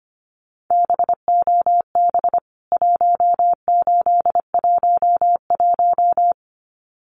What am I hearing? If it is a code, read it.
Morse code, heard as a keyed tone, about 24 wpm, 700 Hz. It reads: BOB 1811
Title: CWT Exchange 25wpm